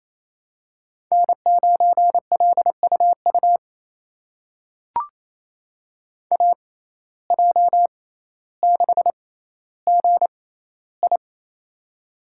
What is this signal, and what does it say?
Morse code, heard as a keyed tone, about 28 wpm, spaced as for 14 wpm, 700 Hz. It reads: N9LUU AJ6GI
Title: USA Call Signs - 5 Characters - 6x Character Spacing - 28wpm